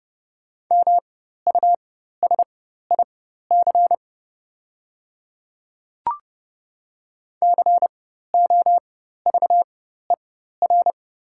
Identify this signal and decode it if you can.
Morse code, heard as a keyed tone, about 30 wpm, spaced as for 14 wpm, 700 Hz. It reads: MUSIC COVER
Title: 4x Character Spacing - Top 300 Words - 30wpm